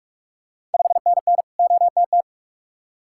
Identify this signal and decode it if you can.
Morse code, heard as a keyed tone, about 46 wpm, 700 Hz. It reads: 5NN OTT
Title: Ontario QSO Party Exchange 45wpm